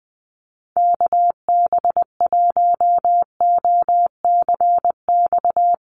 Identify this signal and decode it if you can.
Morse code, heard as a keyed tone, about 19 wpm, 700 Hz. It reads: KB1OCX
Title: USA Call Signs Repeated 3x 20wpm